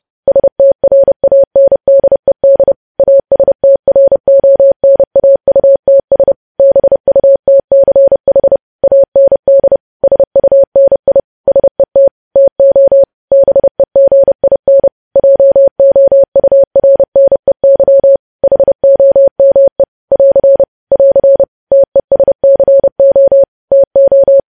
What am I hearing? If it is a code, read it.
Morse code, heard as a keyed tone, about 30 wpm, 550 Hz. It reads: STRANDED ASTRONAUTS BUTCH AND SUNI SET TO BEGIN JOURNEY HOME <AR> <AR> TESCO TO
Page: news headlines in morse code at 30 wpm